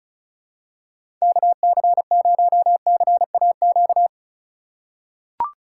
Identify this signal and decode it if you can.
Morse code, heard as a keyed tone, about 34 wpm, 700 Hz. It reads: KC0CAQ
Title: USA Call Signs - 6 Characters - 4x Character Spacing - 35wpm